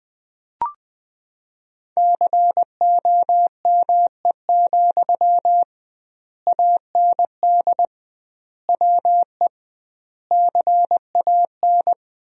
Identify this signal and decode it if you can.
Morse code, heard as a keyed tone, about 21 wpm, 700 Hz. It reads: COME, AND WE CAN
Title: 2x Word-Spacing - Sentences from Top 500 Words - 20wpm